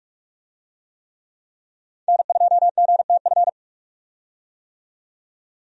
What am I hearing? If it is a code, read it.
Morse code, heard as a keyed tone, about 43 wpm, 700 Hz. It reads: N2GTF
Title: USA Call Signs - 5 Characters - 45wpm